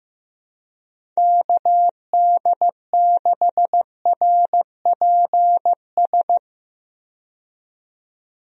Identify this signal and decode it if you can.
Morse code, heard as a keyed tone, about 15 wpm, 700 Hz. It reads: KD6RPS
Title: USA Call Signs - 6 Characters - 15wpm